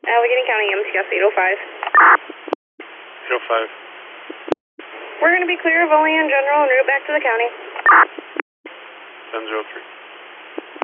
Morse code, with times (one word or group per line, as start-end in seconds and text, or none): none